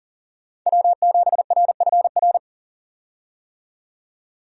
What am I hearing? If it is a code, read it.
Morse code, heard as a keyed tone, about 40 wpm, 700 Hz. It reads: W7RFR
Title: USA Call Signs - 5 Characters - 40wpm